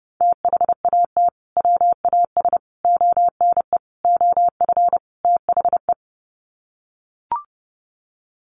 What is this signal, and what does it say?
Morse code, heard as a keyed tone, about 30 wpm, 700 Hz. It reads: THAT WAS ONE OF THE
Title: Sets of 5 Words 30wpm